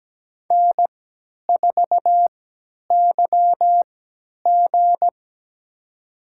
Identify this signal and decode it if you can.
Morse code, heard as a keyed tone, about 17 wpm, spaced as for 10 wpm, 700 Hz. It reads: N4YG
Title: USA Call Signs - 3x Character Spacing - 17wpm